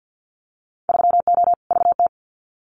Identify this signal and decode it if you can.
Morse code, heard as a keyed tone, about 50 wpm, 700 Hz. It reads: 3O VT
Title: Winter Field Day Exchanges Encore 50wpm